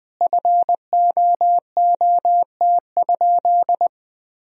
Morse code, heard as a keyed tone, about 20 wpm, 700 Hz. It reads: FOOT?
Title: Sentences from Top 500 Words and Abbreviations 20wpm